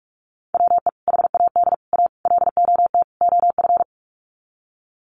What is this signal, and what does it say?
Morse code, heard as a keyed tone, about 44 wpm, 700 Hz. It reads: WE HAD A LOT OF
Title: Sets of 5 Words Encore 45wpm